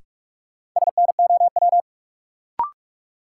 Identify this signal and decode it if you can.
Morse code, heard as a keyed tone, about 47 wpm, 700 Hz. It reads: SNOW
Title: Sentences from Top 500 Words - New Word Drill - 45wpm